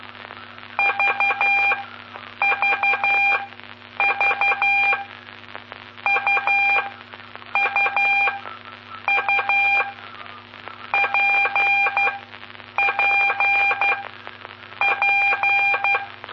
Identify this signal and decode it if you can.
Morse code, heard as a keyed tone, about 12 wpm, spaced as for 7 wpm, 2400 Hz, with noise. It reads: VVV UUU PPP
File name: pp5bar / aulas de telegrafia / Aula-09